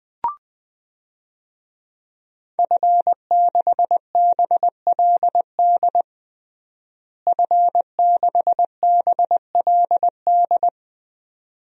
Morse code, heard as a keyed tone, about 20 wpm, 700 Hz. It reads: F6BLD F6BLD
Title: DXCC-100 Call Signs - 5 Characters - Repeated 2x - 20wpm